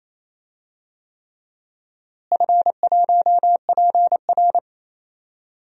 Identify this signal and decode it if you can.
Morse code, heard as a keyed tone, about 27 wpm, 700 Hz. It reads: F1PR
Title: DXCC-100 Call Signs - 4 Characters - Repeated 3x - 28wpm